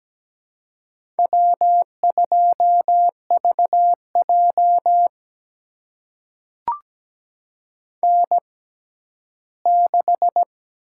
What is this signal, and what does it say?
Morse code, heard as a keyed tone, about 17 wpm, spaced as for 11 wpm, 700 Hz. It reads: W2VJ N6
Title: USA Call Signs - 4 Characters - 6x Character Spacing - 17wpm